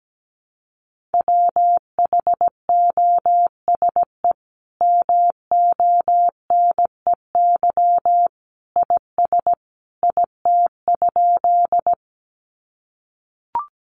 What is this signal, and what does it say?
Morse code, heard as a keyed tone, about 17 wpm, 700 Hz. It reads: WHOSE MONEY IS IT?
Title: Sentences from Top 1200 Words 17wpm